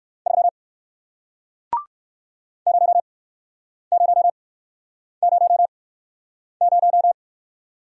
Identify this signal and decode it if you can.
Morse code, heard as a keyed tone, about 60 wpm, spaced as for 9 wpm, 700 Hz. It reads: 3 0000